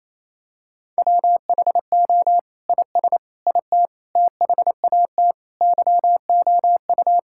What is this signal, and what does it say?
Morse code, heard as a keyed tone, about 28 wpm, 700 Hz. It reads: WHO IS IT THAT YOU
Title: Sentences from Top 1600 Words 28wpm